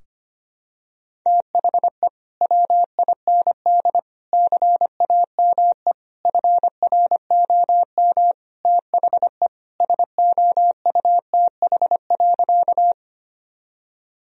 Morse code, heard as a keyed tone, about 25 wpm, 700 Hz. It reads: THE WIND CAME FROM THE SOUTH.
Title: Sentences from Top 1000 Words - Review Part 2 - 25wpm